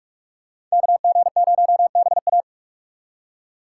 Morse code, heard as a keyed tone, about 46 wpm, 700 Hz. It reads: KG0BA
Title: USA Call Signs - 2x Character Spacing - 45wpm